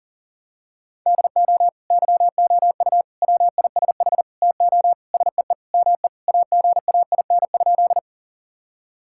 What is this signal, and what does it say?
Morse code, heard as a keyed tone, about 40 wpm, 700 Hz. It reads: DO YOU WISH TO SEE ME AGAIN?
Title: Sentences from Top 1600 Words 40wpm